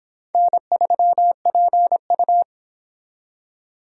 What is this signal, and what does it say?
Morse code, heard as a keyed tone, about 31 wpm, 700 Hz. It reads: N3PU